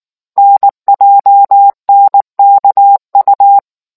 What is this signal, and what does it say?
Morse code, heard as a keyed tone, about 19 wpm, 800 Hz. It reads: NJNKU